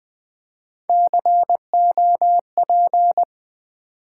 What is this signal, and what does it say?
Morse code, heard as a keyed tone, about 20 wpm, 700 Hz. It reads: COP
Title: Build A Word - 1 Letter At A Time - 3-Letter Words - 20wpm